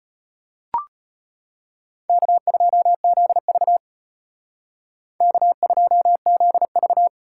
Morse code, heard as a keyed tone, about 32 wpm, 700 Hz. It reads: K2ZV K2ZV